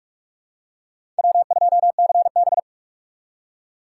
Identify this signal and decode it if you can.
Morse code, heard as a keyed tone, about 47 wpm, 700 Hz. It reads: W2CB
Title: USA Call Signs - 6x Character Spacing - 45wpm